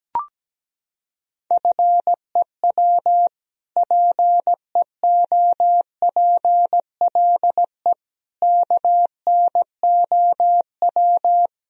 Morse code, heard as a keyed tone, about 17 wpm, 700 Hz. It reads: FEW PEOPLE KNOW
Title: Sentences from Top 1000 Words - Review Part 1 - 17wpm